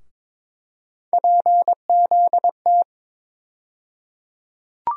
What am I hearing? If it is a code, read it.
Morse code, heard as a keyed tone, about 22 wpm, 700 Hz. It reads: PZT
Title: USA Call Sign Postfixes - 3 Letters 22wpm